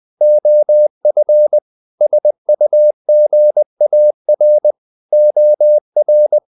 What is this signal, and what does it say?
Morse code, heard as a keyed tone, about 20 wpm, 600 Hz. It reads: OF SUGAR OR
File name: the old man and the sea part 2 20wpm